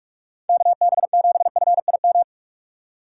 Morse code, heard as a keyed tone, about 64 wpm, spaced as for 54 wpm, 700 Hz. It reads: KB7FIM
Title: Speed Racing - USA 2x3 Call Signs 45wpm